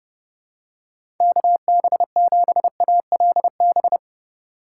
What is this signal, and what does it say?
Morse code, heard as a keyed tone, about 30 wpm, 700 Hz. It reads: KB7ALB
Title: USA Call Signs - 6 Characters - 2x Character Spacing - 30wpm